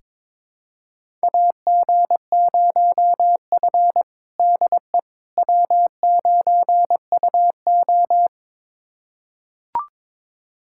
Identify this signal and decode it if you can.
Morse code, heard as a keyed tone, about 22 wpm, 700 Hz. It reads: AG0F DE W9UO
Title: USA Callsign DE Callsign - 4 Characters - 22wpm